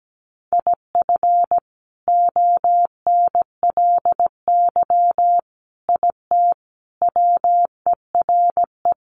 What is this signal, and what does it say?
Morse code, heard as a keyed tone, about 17 wpm, 700 Hz. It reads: IF ONLY IT WERE
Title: Sentences from Top 500 Words 17wpm